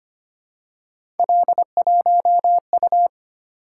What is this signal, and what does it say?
Morse code, heard as a keyed tone, about 25 wpm, 700 Hz. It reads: L1U
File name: Three Letter-Number Combinations - 25wpm